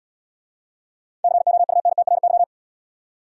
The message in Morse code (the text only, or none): COUNTRY